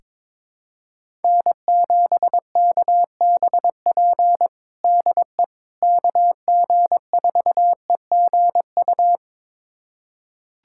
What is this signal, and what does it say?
Morse code, heard as a keyed tone, about 21 wpm, 700 Hz. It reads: N7KBP DE KG4EGU
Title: USA Callsign DE Callsign 22wpm